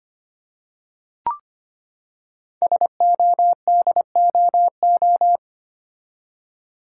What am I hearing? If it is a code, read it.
Morse code, heard as a keyed tone, about 25 wpm, 700 Hz. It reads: SODOO
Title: Code Groups - 5 Letters - 25wpm